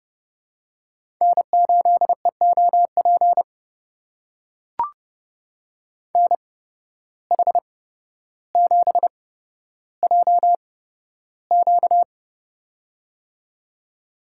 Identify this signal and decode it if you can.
Morse code, heard as a keyed tone, about 30 wpm, spaced as for 13 wpm, 700 Hz. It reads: N8EOP NH7JQ
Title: USA Call Signs - 5 Characters - 8x Character Spacing - 30wpm